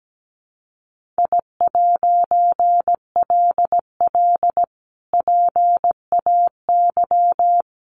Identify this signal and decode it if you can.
Morse code, heard as a keyed tone, about 17 wpm, 700 Hz. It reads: I'LL PAY
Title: A Sound Of Thunder 17wpm